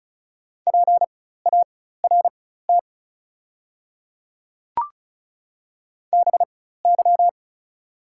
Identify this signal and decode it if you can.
Morse code, heard as a keyed tone, about 34 wpm, spaced as for 16 wpm, 700 Hz. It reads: PART BY
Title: 4x Character Spacing - Top 200 Words - 35wpm